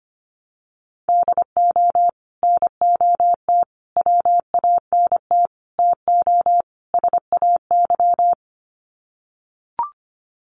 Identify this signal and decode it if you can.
Morse code, heard as a keyed tone, about 25 wpm, 700 Hz. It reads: DO NOT WANT TO SAY
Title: Sets of 5 Words Encore 25wpm